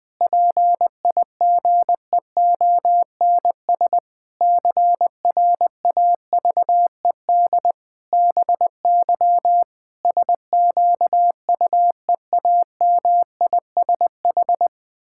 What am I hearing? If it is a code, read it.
Morse code, heard as a keyed tone, about 21 wpm, 700 Hz. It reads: PIGEONS CRAVED BY SQUEAMISH
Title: Pangram Challenge v2 - 20wpm